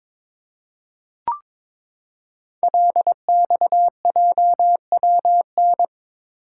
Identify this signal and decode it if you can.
Morse code, heard as a keyed tone, about 22 wpm, 700 Hz. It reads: LXJWN
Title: Code Groups - 5 Letter:Numbers - 22wpm